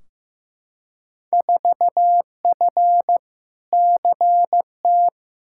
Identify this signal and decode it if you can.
Morse code, heard as a keyed tone, about 15 wpm, 700 Hz. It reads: 4F CT
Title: Field Day Exchanges Encore - 15wpm